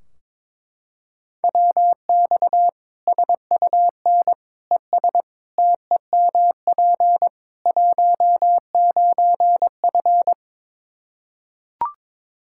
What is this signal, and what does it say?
Morse code, heard as a keyed tone, about 22 wpm, 700 Hz. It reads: WX SUN ES TEMP 19F
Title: Practice QSO - Exchange 2 - Info - 22wpm